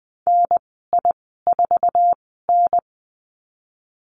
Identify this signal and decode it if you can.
Morse code, heard as a keyed tone, about 20 wpm, spaced as for 15 wpm, 700 Hz. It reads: NI4N
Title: 048 Lesson 9g - Number 4 - Call Signs 20wpm